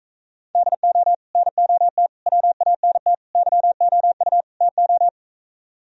Morse code, heard as a keyed tone, about 46 wpm, 700 Hz. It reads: DO NOT WANT YOU TO